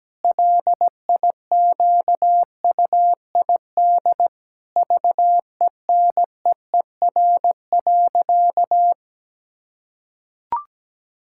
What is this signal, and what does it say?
Morse code, heard as a keyed tone, about 17 wpm, 700 Hz. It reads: LIQUID VENEER.
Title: Pangram Challenge v2 - 17wpm